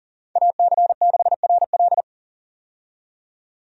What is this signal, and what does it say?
Morse code, heard as a keyed tone, about 40 wpm, 700 Hz. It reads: AC6RL
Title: USA Call Signs - 5 Characters - 40wpm